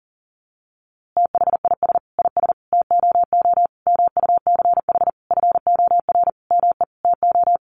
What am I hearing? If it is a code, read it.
Morse code, heard as a keyed tone, about 40 wpm, 700 Hz. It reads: THIS IS TOO MUCH FOR ME TO
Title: A Sound Of Thunder 40wpm